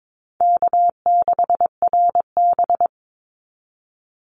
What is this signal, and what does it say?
Morse code, heard as a keyed tone, about 22 wpm, 700 Hz. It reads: K6RB
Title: Easy Call Signs Repeated 3x - 22wpm